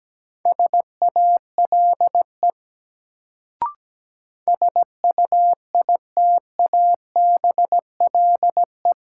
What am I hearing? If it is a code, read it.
Morse code, heard as a keyed tone, about 17 wpm, 700 Hz. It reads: SALE SUITABLE
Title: Sentences from Top 1400 Words - New Word Drill Alt - 17wpm